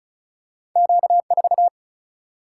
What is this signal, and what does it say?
Morse code, heard as a keyed tone, about 36 wpm, 700 Hz. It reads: Q 4
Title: Single-Letter Single-Number 35wpm